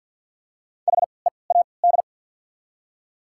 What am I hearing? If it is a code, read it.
Morse code, heard as a keyed tone, about 53 wpm, 690 Hz. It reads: H E A D
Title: 3x Character Spacing - Top 300 Words - 50wpm